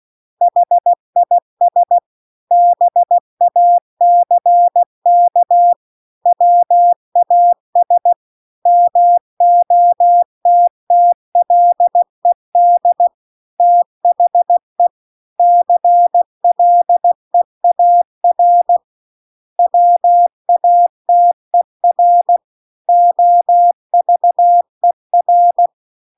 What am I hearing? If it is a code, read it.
Morse code, heard as a keyed tone, about 16 wpm, 700 Hz. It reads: HIS BACK WAS MOTTLED THE CLEAR WATER OVER
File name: CW-The-Big-Twoheaded-River-2-16WPM